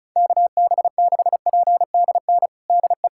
Morse code, heard as a keyed tone, about 34 wpm, 700 Hz. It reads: KB6PDN DE
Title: USA Callsign DE Callsign - 6 Characters - 35wpm